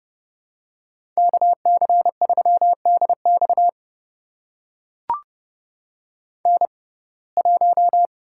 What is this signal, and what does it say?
Morse code, heard as a keyed tone, about 31 wpm, spaced as for 22 wpm, 700 Hz. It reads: KC3DX N1
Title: USA Call Signs - 5 Characters - 6x Character Spacing - 30wpm